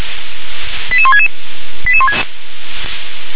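Five-tone selective call (ZVEI), heard as quick runs of five tones, tones of about 70 ms. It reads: 88166
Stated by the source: ZVEI five-tone selective call received on an SDR.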